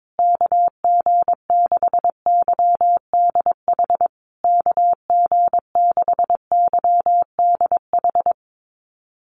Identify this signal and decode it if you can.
Morse code, heard as a keyed tone, about 22 wpm, 700 Hz. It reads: KG6YDH KG6YDH
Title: USA Call Signs Repeated 3x 22wpm